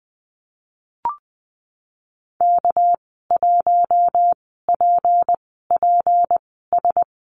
Morse code, heard as a keyed tone, about 20 wpm, spaced as for 15 wpm, 700 Hz. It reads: K1PPS
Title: 219 Lesson 35g - Number 0 - Call Signs 20wpm